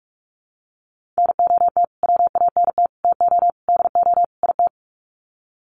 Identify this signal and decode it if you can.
Morse code, heard as a keyed tone, about 44 wpm, 700 Hz. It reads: NOT WANT TO DO IT
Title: Sets of 5 Words Encore 45wpm